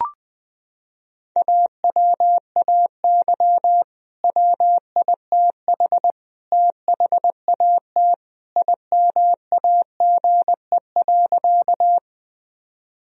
Morse code, heard as a keyed tone, about 20 wpm, 700 Hz. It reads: AWAY WITH THAT IMAGE.
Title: Sentences from Top 1000 Words - Review Part 4 - 20wpm